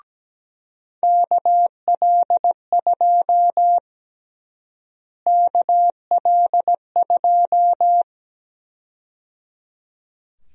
Morse code, heard as a keyed tone, about 17 wpm, 700 Hz. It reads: KL2 KL2
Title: USA Call Sign Prefixes - Repeated 2x - 17wpm